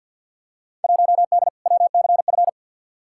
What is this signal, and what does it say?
Morse code, heard as a keyed tone, about 47 wpm, 700 Hz. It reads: 1D WCF
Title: Field Day Exchanges - 50wpm